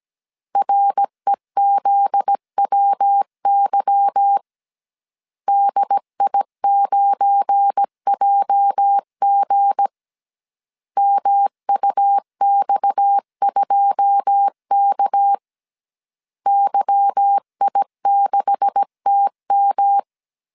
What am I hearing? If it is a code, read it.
Morse code, heard as a keyed tone, about 17 wpm, 800 Hz. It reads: REZWY DI9JG MUX2K YI6TM
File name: CW / DET / 1 9 1500C S C 16wpm 2909